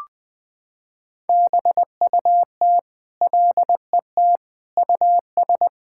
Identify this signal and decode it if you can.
Morse code, heard as a keyed tone, about 20 wpm, 700 Hz. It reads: BUT LET US
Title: Sentences from Top 1600 Words 20wpm